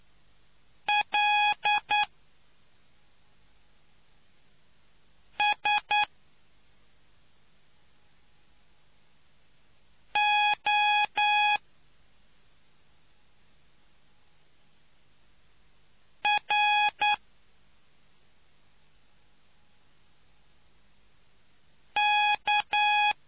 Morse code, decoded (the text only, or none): LSORK